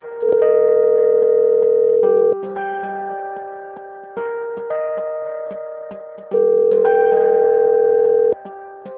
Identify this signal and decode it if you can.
Ringback tone (North America), 440 + 480 Hz.
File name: rg-1600-unknown- - - .56225